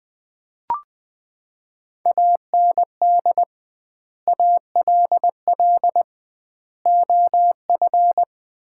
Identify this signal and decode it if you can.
Morse code, heard as a keyed tone, about 21 wpm, 700 Hz. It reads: AND ALL OF
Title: 2x Word-Spacing - Sets of 5 Words - 20wpm